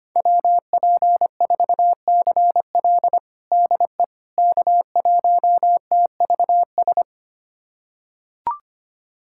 Morse code, heard as a keyed tone, about 25 wpm, 700 Hz. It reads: WP4CL DE K1TVS
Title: USA Callsign DE Callsign - 5 Characters - 25wpm